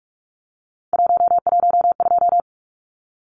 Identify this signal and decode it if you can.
Morse code, heard as a keyed tone, about 44 wpm, 700 Hz. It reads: 112
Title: Numbers - 3-Digit 45wpm